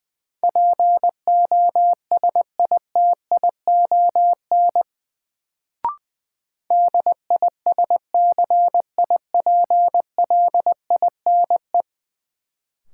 Morse code, heard as a keyed tone, about 20 wpm, 700 Hz. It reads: POSITION DISCIPLINE
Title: Sentences from Top 900 Words - New Word Drill - 20wpm